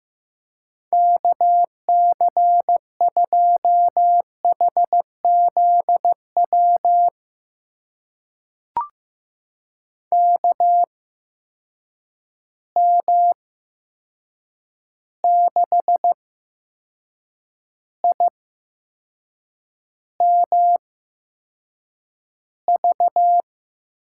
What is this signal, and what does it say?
Morse code, heard as a keyed tone, about 15 wpm, spaced as for 6 wpm, 700 Hz. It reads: KC2HZW KM6IMV
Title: USA Call Signs - 6 Characters - 8x Character Spacing - 15wpm